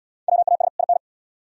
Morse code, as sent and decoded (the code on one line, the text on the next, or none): .-- .- ... / .. -.
WAS IN